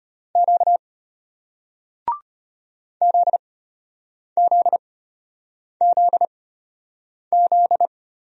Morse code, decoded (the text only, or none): Q ZZZZ